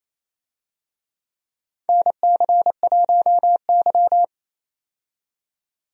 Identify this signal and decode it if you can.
Morse code, heard as a keyed tone, about 27 wpm, 700 Hz. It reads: NC1Y
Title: USA Call Signs - 4 Characters - 28wpm